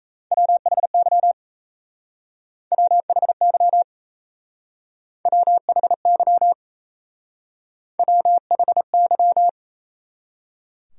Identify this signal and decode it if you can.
Morse code, heard as a keyed tone, about 34 wpm, 700 Hz. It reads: WHY WHY WHY WHY
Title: Speed-Racing Top 500 Words 28wpm